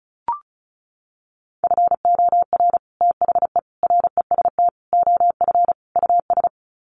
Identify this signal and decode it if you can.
Morse code, heard as a keyed tone, about 34 wpm, 700 Hz. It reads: FOR THE REST OF US
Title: Sets of 5 Words Encore 35wpm